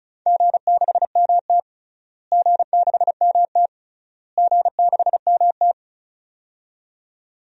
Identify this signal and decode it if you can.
Morse code, heard as a keyed tone, about 36 wpm, 700 Hz. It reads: G6MT G6MT G6MT
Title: DXCC-100 Call Signs - 4 Characters - Repeated 3x - 35wpm